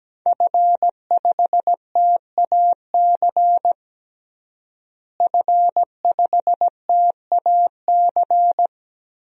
Morse code, heard as a keyed tone, about 17 wpm, 700 Hz. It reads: F5TAC F5TAC
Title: DXCC-100 Call Signs - 5 Characters - Repeated 2x - 17wpm